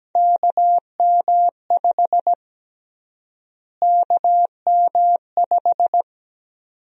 Morse code, heard as a keyed tone, about 17 wpm, 700 Hz. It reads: KM5 KM5
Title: USA Call Sign Prefixes - Repeated 2x - 17wpm